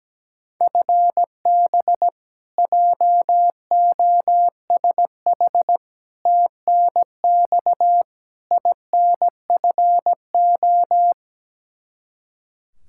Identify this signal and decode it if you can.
Morse code, heard as a keyed tone, about 17 wpm, 700 Hz. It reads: FB JOSH TNX INFO
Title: Practice QSO - Exchange 3 - More Info - 17wpm